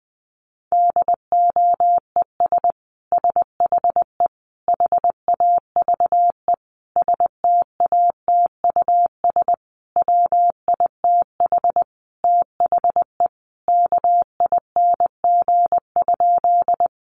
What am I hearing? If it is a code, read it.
Morse code, heard as a keyed tone, about 20 wpm, 700 Hz. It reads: DOES SHE HAVE STATUS WITH THE KING?
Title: Sentences from Top 1400 Words 20wpm